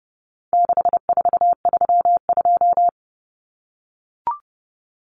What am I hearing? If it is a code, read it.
Morse code, heard as a keyed tone, about 30 wpm, 700 Hz. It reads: 6432
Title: Common Cut Numbers - 4-Digit 30wpm